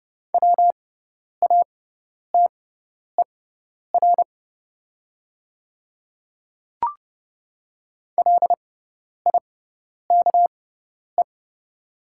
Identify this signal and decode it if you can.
Morse code, heard as a keyed tone, about 30 wpm, spaced as for 10 wpm, 700 Hz. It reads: WATER LIKE
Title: 6x Character Spacing - Top 100 Words - 30wpm